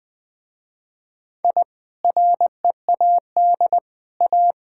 Morse code, heard as a keyed tone, about 20 wpm, 700 Hz. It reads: I READ A
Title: Sentences with 4-Word Phrases - Spoken Beginning - 20wpm